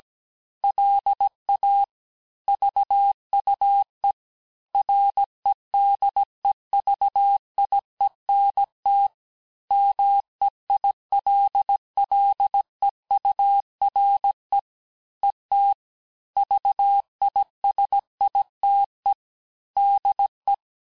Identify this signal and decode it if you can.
Morse code, heard as a keyed tone, about 18 wpm, 800 Hz. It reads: LA VUE REDEVIENT MEILLEURE ET VISITE DE
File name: cours-du-lundi files / cours-1587-FIN-de-QSO-vitesse-16MPM